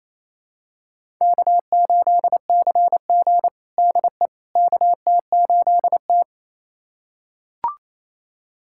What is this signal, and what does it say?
Morse code, heard as a keyed tone, about 27 wpm, 700 Hz. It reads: K8CG DE KT8T
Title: USA Callsign DE Callsign - 4 Characters - 28wpm